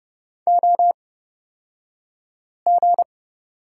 Morse code, heard as a keyed tone, about 30 wpm, spaced as for 5 wpm, 700 Hz. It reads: OG
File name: Lesson 38d - Letter q - Two Characters 30wpm